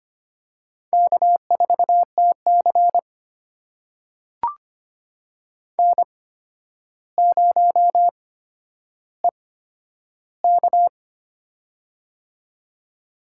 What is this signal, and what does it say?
Morse code, heard as a keyed tone, about 25 wpm, spaced as for 11 wpm, 700 Hz. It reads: K4TC N0EK
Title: USA Call Signs - 4 Characters - 8x Character Spacing - 25wpm